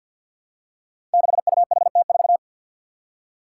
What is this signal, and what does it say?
Morse code, heard as a keyed tone, about 50 wpm, 700 Hz. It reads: 6UHT4
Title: Five Challenging Letter-Number Combinations 50wpm